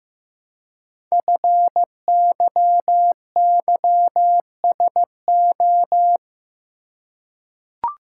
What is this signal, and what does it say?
Morse code, heard as a keyed tone, about 15 wpm, 700 Hz. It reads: FYYSO
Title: Code Groups - 5 Letters - 15wpm